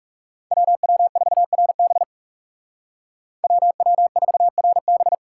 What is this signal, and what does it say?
Morse code, heard as a keyed tone, about 37 wpm, 700 Hz. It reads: WW4RB WW4RB
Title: Speed Racing - USA 2x2 Call Signs 30wpm